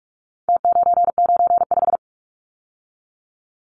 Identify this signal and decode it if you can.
Morse code, heard as a keyed tone, about 44 wpm, 700 Hz. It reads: T995
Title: Numbers - Contest Style - 4-Digit 45wpm